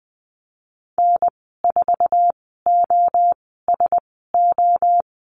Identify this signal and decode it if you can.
Morse code, heard as a keyed tone, about 20 wpm, spaced as for 15 wpm, 700 Hz. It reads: N4OSO
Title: 048 Lesson 9g - Number 4 - Call Signs 20wpm